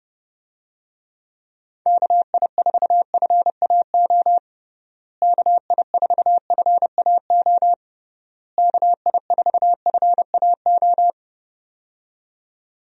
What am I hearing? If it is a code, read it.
Morse code, heard as a keyed tone, about 32 wpm, 700 Hz. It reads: KI4FAO KI4FAO KI4FAO
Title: USA Call Signs - 6 Characters - Repeated 3x - 30wpm